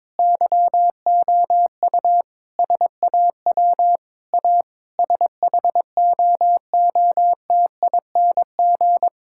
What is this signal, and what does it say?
Morse code, heard as a keyed tone, about 22 wpm, 700 Hz. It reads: YOU SAW A SHOOTING
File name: practice / 22wpm / Sentences with 8-Word Phrases - 22wpm